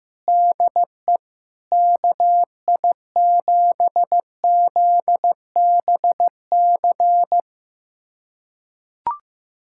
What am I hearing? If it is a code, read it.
Morse code, heard as a keyed tone, about 15 wpm, 700 Hz. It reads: DE KI7ZBC
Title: USA Callsign DE Callsign 15wpm